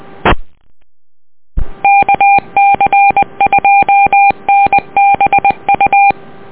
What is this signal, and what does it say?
Morse code, heard as a keyed tone, about 20 wpm, 800 Hz. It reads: E KC2NBU
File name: net audio